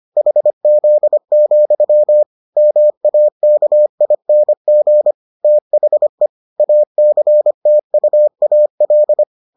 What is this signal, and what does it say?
Morse code, heard as a keyed tone, about 25 wpm, 600 Hz. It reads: HZ, MAKING THE ACTUAL